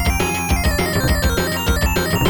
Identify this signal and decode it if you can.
There is no code.